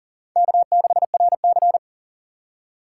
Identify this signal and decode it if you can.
Morse code, heard as a keyed tone, about 40 wpm, 700 Hz. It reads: K6RC
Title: K1USN SST QSO Practice - 40wpm